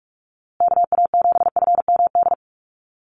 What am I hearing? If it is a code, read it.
Morse code, heard as a keyed tone, about 44 wpm, 700 Hz. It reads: KA7FMD
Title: USA Call Signs Repeated 2x 45wpm